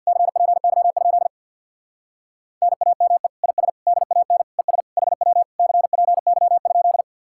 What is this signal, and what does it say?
Morse code, heard as a keyed tone, about 57 wpm, 700 Hz. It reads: CPY? NAME IS DAN ES HW CPY?
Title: Speed-Racing Common QSO Phrases - 50wpm